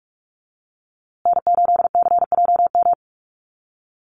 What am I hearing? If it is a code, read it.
Morse code, heard as a keyed tone, about 44 wpm, 690 Hz. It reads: N8CJM
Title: USA Call Signs Repeated 3x 45wpm